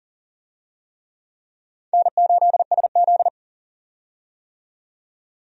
Morse code, heard as a keyed tone, about 40 wpm, 700 Hz. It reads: N8SZ
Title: USA Call Signs - 4 Characters - 40wpm